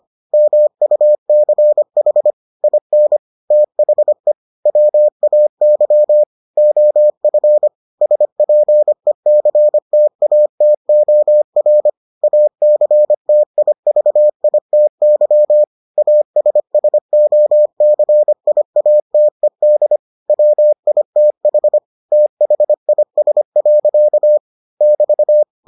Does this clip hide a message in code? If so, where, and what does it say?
Yes: Morse code from 0.3 to 25.4 s, heard as a keyed tone, about 25 wpm, 600 Hz. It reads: MUCH IN THE WAY OF SPECTATOR ACTIVITY ASSOCIATED WITH THIS. <BT>